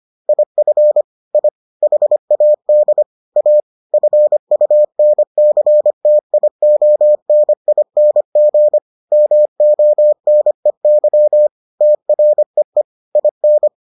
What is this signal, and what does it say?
Morse code, heard as a keyed tone, about 25 wpm, 600 Hz. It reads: IF I HAD A FUNCTIONING MONEY TREE IN